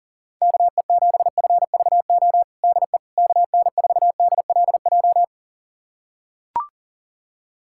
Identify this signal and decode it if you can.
Morse code, heard as a keyed tone, about 37 wpm, 690 Hz. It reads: KE7FVO DE KN4DLJ
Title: USA Callsign DE Callsign - 6 Characters - 40wpm